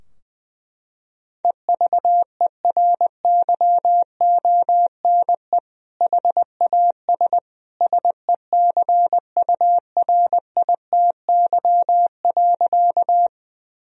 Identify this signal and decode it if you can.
Morse code, heard as a keyed tone, about 21 wpm, 700 Hz. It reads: EVERYONE HAS SECURITY.
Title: Sentences from Top 1000 Words - Review Part 4 - 20wpm